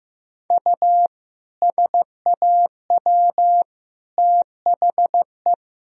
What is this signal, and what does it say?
Morse code, heard as a keyed tone, about 16 wpm, 700 Hz. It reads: U SAW THE
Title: Sentences from Top 400 Words and Abbreviations 15wpm